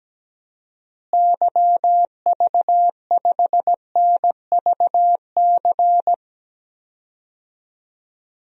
Morse code, heard as a keyed tone, about 17 wpm, 700 Hz. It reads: YV5NVC
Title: DXCC-100 Call Signs - 6 Characters - Repeated 3x - 17wpm